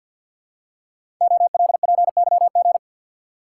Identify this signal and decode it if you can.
Morse code, heard as a keyed tone, about 48 wpm, 700 Hz. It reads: OLPYG
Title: Code Groups - 5 Letters - 50wpm